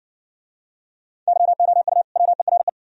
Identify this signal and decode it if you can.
Morse code, heard as a keyed tone, about 60 wpm, 700 Hz. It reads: YOU WERE